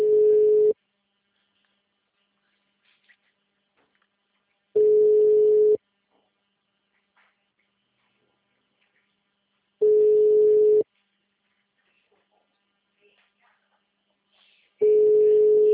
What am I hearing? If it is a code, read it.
Ringback tone (Europe, 425 Hz).